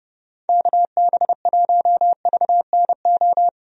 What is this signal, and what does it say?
Morse code, heard as a keyed tone, about 30 wpm, 700 Hz. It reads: KB1VNO
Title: USA Call Signs - 6 Characters - Repeated 3x - 30wpm